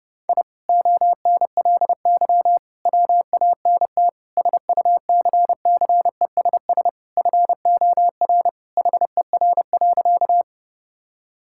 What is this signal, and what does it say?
Morse code, heard as a keyed tone, about 30 wpm, 700 Hz. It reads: I ONLY WANT SUCCESS FOR HER.
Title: Sentences from Top 1000 Words - Review Part 6 - 30wpm